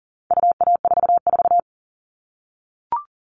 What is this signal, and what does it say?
Morse code, heard as a keyed tone, about 40 wpm, 700 Hz. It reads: UA44
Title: All Cut Numbers - 4-Digit 40wpm